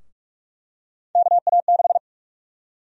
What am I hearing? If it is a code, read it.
Morse code, heard as a keyed tone, about 46 wpm, 700 Hz. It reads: KA6
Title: USA Call Sign Prefixes - 45wpm